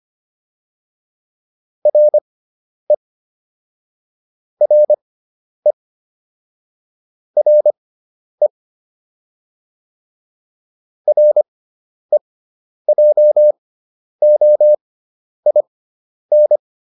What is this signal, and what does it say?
Morse code, heard as a keyed tone, about 25 wpm, spaced as for 10 wpm, 600 Hz. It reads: RE RE RE REJOIN